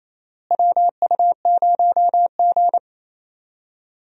Morse code, heard as a keyed tone, about 28 wpm, 700 Hz. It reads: WU0G
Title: USA Call Signs - 4 Characters - Repeated 2x - 28wpm